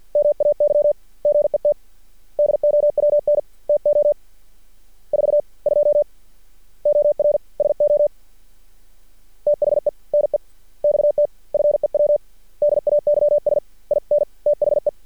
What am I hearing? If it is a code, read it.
Morse code, heard as a keyed tone, about 53 wpm, spaced as for 44 wpm, 590 Hz. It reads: MAY GET DOWN TO 42 OR SO THE NEXT FEW DAYS IN THE
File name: archived / CRSnet